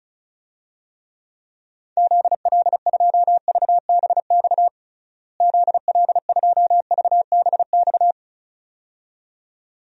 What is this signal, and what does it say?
Morse code, heard as a keyed tone, about 36 wpm, 700 Hz. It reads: ZL2VBX ZL2VBX
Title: DXCC-100 Call Signs - 6 Characters - Repeated 2x - 35wpm